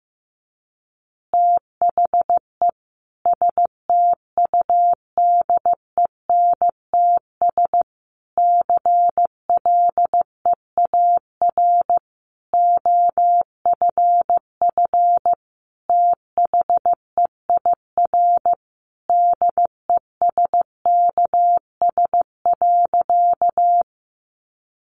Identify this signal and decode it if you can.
Morse code, heard as a keyed tone, about 15 wpm, 700 Hz. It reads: THE STUDENTS CLEAR OFF THEIR DESKS.
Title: Additional Easy Phrases 15wpm